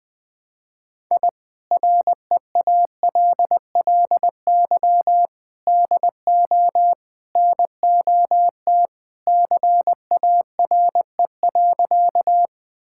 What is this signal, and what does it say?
Morse code, heard as a keyed tone, about 20 wpm, 700 Hz. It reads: I REALLY DO NOT CARE.
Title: Sentences with 4-Word Phrases - 20wpm